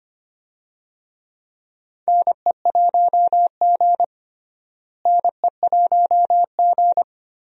Morse code, heard as a keyed tone, about 25 wpm, 700 Hz. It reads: NE1G NE1G
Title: USA Call Signs - 4 Characters - Repeated 2x - 25wpm